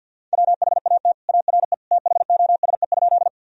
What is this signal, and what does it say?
Morse code, heard as a keyed tone, about 51 wpm, 700 Hz. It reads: WHAT ARE THOSE?
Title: Sentences from Top 1000 Words - Review Part 1 - 50wpm